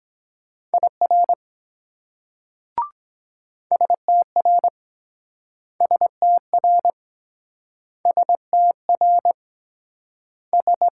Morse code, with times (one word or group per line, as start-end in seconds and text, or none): 0.7-1.3 IR
3.7-4.7 STR
5.8-6.9 STR
8.1-9.3 STR
10.5-10.9 S